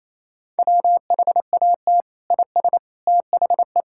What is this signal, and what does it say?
Morse code, heard as a keyed tone, about 28 wpm, 700 Hz. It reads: WHAT IS THE
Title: Sentences from Top 1000 Words - Review Part 6 - 28wpm